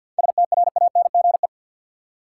Morse code, heard as a keyed tone, about 50 wpm, 700 Hz. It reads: STRANGE
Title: Seven Letter Words - 50wpm